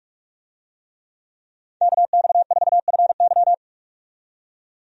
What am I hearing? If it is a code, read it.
Morse code, heard as a keyed tone, about 46 wpm, 700 Hz. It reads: KX4FY
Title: USA Call Signs - 5 Characters - 45wpm